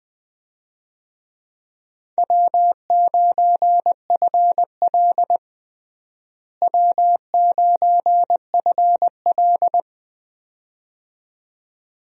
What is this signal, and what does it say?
Morse code, heard as a keyed tone, about 21 wpm, 700 Hz. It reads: W9FL W9FL
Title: USA Call Signs - 4 Characters - Repeated 2x - 20wpm